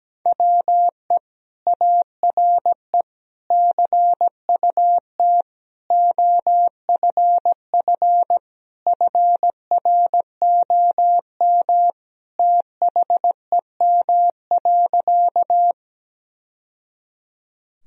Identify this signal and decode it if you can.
Morse code, heard as a keyed tone, about 17 wpm, 700 Hz. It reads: WE ARE CUT OFF FROM THEM.
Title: Sentences from Top 1000 Words - Review Part 1 - 17wpm